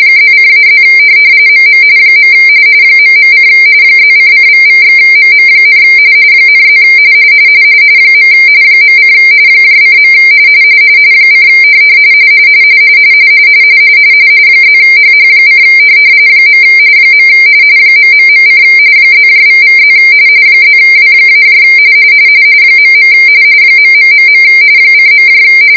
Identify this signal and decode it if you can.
RTTY, about 45.45 baud. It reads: +:-+ MM . ..: +::==/MM, ,+ . .: :. : +HH: . ...: +==+ +W- .. .... :.